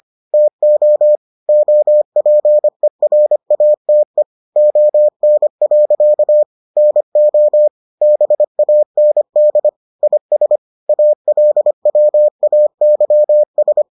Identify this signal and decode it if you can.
Morse code, heard as a keyed tone, about 25 wpm, 600 Hz. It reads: TO OPERATE ON. NO BAND IS ALWAYS